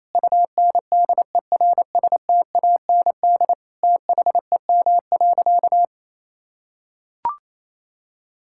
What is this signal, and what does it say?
Morse code, heard as a keyed tone, about 28 wpm, 700 Hz. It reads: UNDERSTAND THEM.
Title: Sentences from Top 1000 Words - Review Part 5 - 28wpm